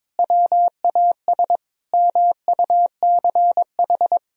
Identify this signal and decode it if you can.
Morse code, heard as a keyed tone, about 22 wpm, 700 Hz. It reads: WAS MUCH
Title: Sentences from Top 1000 Words - Review Part 4 - 22wpm